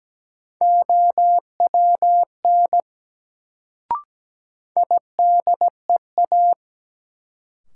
Morse code, heard as a keyed tone, about 17 wpm, 700 Hz. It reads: OWN IDEA
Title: Sentences from Top 300 Words - New Word Drill - 17wpm